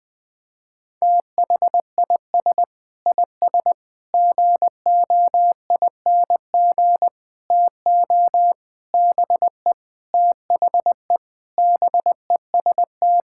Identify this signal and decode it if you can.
Morse code, heard as a keyed tone, about 20 wpm, 700 Hz. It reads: THIS IS GOING TO BE THE BEST
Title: Sentences with 4-Word Phrases - 20wpm